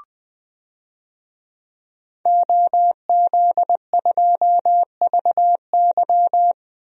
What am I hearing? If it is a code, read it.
Morse code, heard as a keyed tone, about 20 wpm, 700 Hz. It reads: OZ2VY
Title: DXCC-100 Call Signs - 5 Characters - Repeated 3x - 20wpm